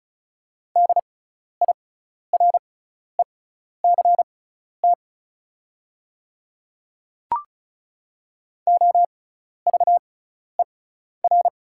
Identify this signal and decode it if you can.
Morse code, heard as a keyed tone, about 36 wpm, spaced as for 12 wpm, 690 Hz. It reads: DIRECT OVER
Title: 6x Character Spacing - Top 500 Words - 35wpm